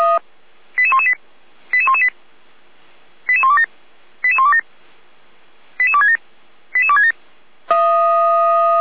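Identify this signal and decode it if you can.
Five-tone selective call (ZVEI), heard as quick runs of five tones, tones of about 70 ms. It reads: 88118 88118 88127 88127 88267 88267
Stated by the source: ZVEI five-tone selective call received on an SDR, from the Feuerwehr Orlamünde fire brigade server.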